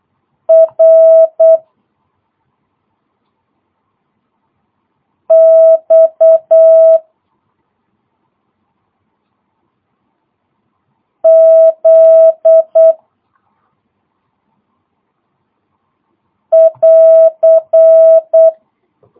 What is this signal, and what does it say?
Morse code, heard as a keyed tone, about 8 wpm, 650 Hz. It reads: R X Z <AR>
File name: jh8cbh / 2022cw